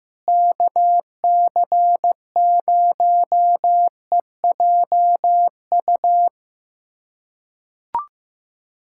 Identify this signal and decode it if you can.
Morse code, heard as a keyed tone, about 16 wpm, 700 Hz. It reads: KC0EJU
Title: USA Callsign DE Callsign - 6 Characters - 15wpm